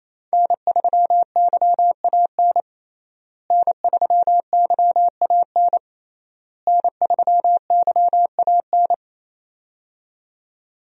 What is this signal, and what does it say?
Morse code, heard as a keyed tone, about 28 wpm, 700 Hz. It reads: N3YAN N3YAN N3YAN
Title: USA Call Signs - 5 Characters - Repeated 3x - 28wpm